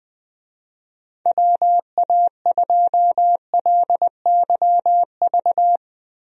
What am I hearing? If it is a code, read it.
Morse code, heard as a keyed tone, about 20 wpm, 700 Hz. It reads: WA2LYV
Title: USA Call Signs - 6 Characters - 20wpm